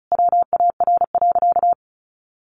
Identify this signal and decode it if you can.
Morse code, heard as a keyed tone, about 34 wpm, 700 Hz. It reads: WAR.
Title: Sentences from Top 400 Words 35wpm